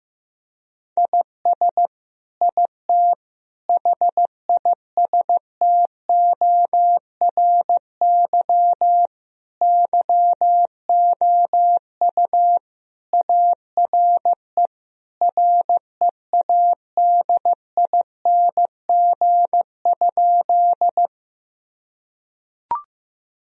Morse code, heard as a keyed tone, about 15 wpm, 700 Hz. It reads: IS IT HISTORY YOU ARE READING?
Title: Sentences from Top 1000 Words - Review Part 3 - 15wpm